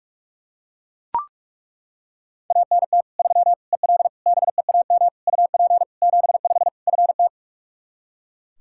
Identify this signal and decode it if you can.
Morse code, heard as a keyed tone, about 41 wpm, 700 Hz. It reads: ANT 3 EL BEAM UP 75 FT
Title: Practice QSO - Exchange 2 - Info - 45wpm